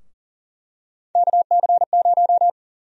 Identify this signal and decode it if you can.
Morse code, heard as a keyed tone, about 40 wpm, 700 Hz. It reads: KC0
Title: USA Call Sign Prefixes - Repeated 2x - 40wpm